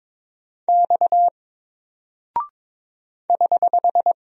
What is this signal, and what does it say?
Morse code, heard as a keyed tone, about 22 wpm, 700 Hz. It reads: X <HH>
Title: Single Character (Letter:Number:Punctuation:Prosign) 22wpm